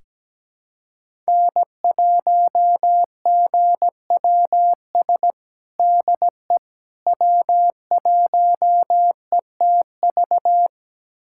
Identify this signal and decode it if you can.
Morse code, heard as a keyed tone, about 17 wpm, 700 Hz. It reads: N1GWS DE W1ETV
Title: USA Callsign DE Callsign - 5 Characters - 17wpm